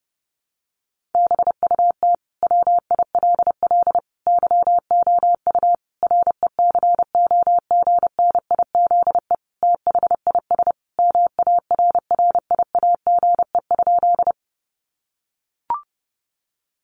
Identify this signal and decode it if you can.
Morse code, heard as a keyed tone, about 30 wpm, 700 Hz. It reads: BUT WILL YOU RECOGNIZE THIS MARRIAGE?
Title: Sentences from Top 1400 Words 30wpm